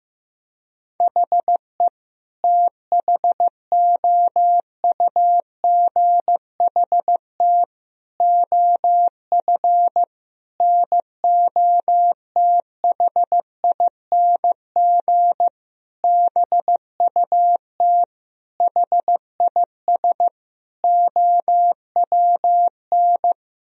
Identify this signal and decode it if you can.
Morse code, heard as a keyed tone, about 15 wpm, 700 Hz. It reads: HE THOUGHT OF NOTHING BUT HIS OWN
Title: Sentences from Top 1000 Words - Review Part 4 - 15wpm